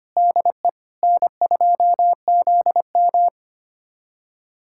Morse code, heard as a keyed tone, about 25 wpm, 700 Hz. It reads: DE N2ZM
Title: USA Callsign DE Callsign 25wpm